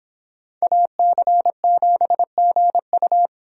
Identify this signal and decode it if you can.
Morse code, heard as a keyed tone, about 31 wpm, 700 Hz. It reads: AC7GU